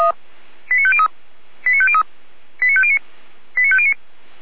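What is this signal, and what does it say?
Five-tone selective call (ZVEI), heard as quick runs of five tones, tones of about 70 ms. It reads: 89503 89503 89509 89509
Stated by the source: ZVEI five-tone selective call received with an SDR.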